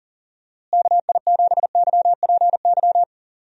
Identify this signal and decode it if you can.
Morse code, heard as a keyed tone, about 37 wpm, 700 Hz. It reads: KI7YPY
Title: USA Call Signs - 6 Characters - Repeated 2x - 40wpm